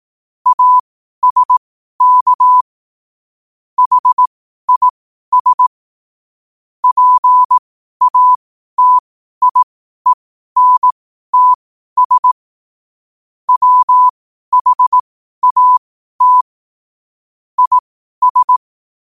Morse code, heard as a keyed tone, about 18 wpm, spaced as for 12 wpm, 1000 Hz. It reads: ASK HIS PATIENTS WHAT IS
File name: qotdfiles / 12wpm